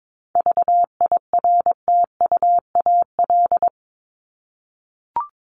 Morse code, heard as a keyed tone, about 22 wpm, 700 Hz. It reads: VIRTUAL
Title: LY Suffix - Top 100 Words with Base Words 22wpm